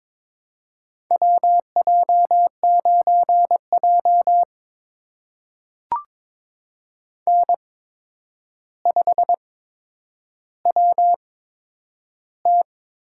Morse code, heard as a keyed tone, about 22 wpm, spaced as for 9 wpm, 700 Hz. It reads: WJ9J N5WT
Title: USA Call Signs - 4 Characters - 8x Character Spacing - 22wpm